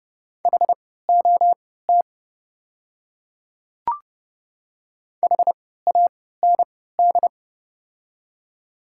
Morse code, heard as a keyed tone, about 29 wpm, spaced as for 17 wpm, 700 Hz. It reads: HOT HAND
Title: 3x Character Spacing - Top 300 Words - 30wpm